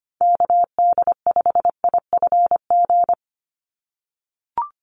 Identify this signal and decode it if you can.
Morse code, heard as a keyed tone, about 25 wpm, 700 Hz. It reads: KD5IFG
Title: USA Call Signs 25wpm